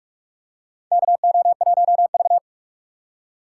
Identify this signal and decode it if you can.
Morse code, heard as a keyed tone, about 47 wpm, 700 Hz. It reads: KO1V
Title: USA Call Signs - 4x Character Spacing - 45wpm